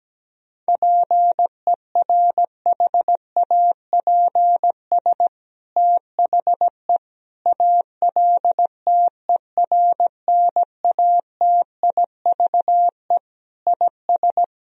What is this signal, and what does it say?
Morse code, heard as a keyed tone, about 17 wpm, 700 Hz. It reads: PERHAPS THE ALTERNATIVE IS
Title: Sentences from Top 1000 Words - Review Part 4 - 17wpm